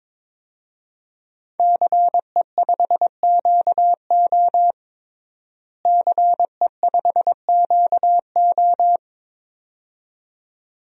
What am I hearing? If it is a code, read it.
Morse code, heard as a keyed tone, about 22 wpm, 700 Hz. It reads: CE5QO CE5QO
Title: DXCC-100 Call Signs - 5 Characters - Repeated 2x - 22wpm